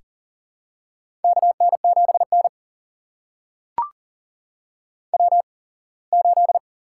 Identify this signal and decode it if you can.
Morse code, heard as a keyed tone, about 40 wpm, spaced as for 24 wpm, 700 Hz. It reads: KN7N W8
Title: USA Call Signs - 4 Characters - 8x Character Spacing - 40wpm